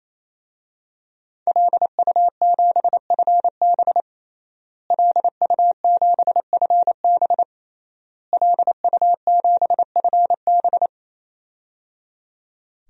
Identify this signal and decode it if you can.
Morse code, heard as a keyed tone, about 28 wpm, 700 Hz. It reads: LU7FB LU7FB LU7FB
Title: DXCC-100 Call Signs - 5 Characters - Repeated 3x - 28wpm